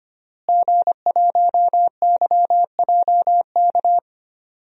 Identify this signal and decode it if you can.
Morse code, heard as a keyed tone, about 25 wpm, 700 Hz. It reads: G1YJK
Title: DXCC-100 Call Signs - 5 Characters - Repeated 2x - 25wpm